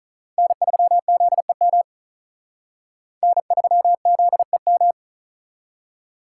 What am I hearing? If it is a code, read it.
Morse code, heard as a keyed tone, about 42 wpm, 700 Hz. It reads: N3ZEM N3ZEM
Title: Speed Racing - USA 1x3 Call Signs 35wpm